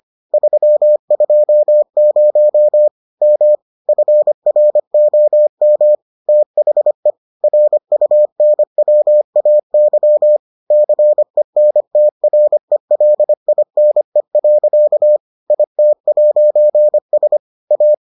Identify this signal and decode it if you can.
Morse code, heard as a keyed tone, about 25 wpm, 600 Hz. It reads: 320 M FROM THE RUNWAY CENTRELINE. IT'S A